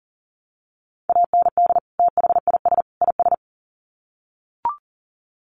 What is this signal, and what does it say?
Morse code, heard as a keyed tone, about 40 wpm, 700 Hz. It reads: AND THIS IS
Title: Sets of 3 Words 40wpm